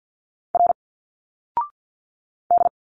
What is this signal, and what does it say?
Morse code, heard as a keyed tone, about 50 wpm, 700 Hz. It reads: R D
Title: Single Letters 50wpm